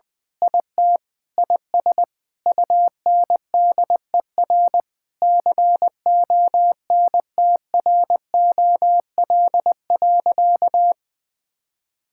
Morse code, heard as a keyed tone, about 21 wpm, 700 Hz. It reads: IT IS UNDER CONTROL.
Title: Sentences with 9-Word Phrases - 20wpm